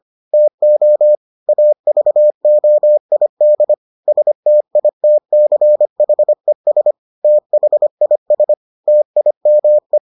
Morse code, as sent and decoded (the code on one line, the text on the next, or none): - --- / .- ...- --- .. -.. / ... - .. - -.-. .... ... / - .... .. ... / - .. -- .
TO AVOID STITCHES THIS TIME